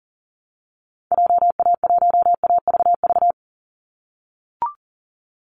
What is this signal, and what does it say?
Morse code, heard as a keyed tone, about 40 wpm, 700 Hz. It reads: JA1AVV
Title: USA Call Signs 40wpm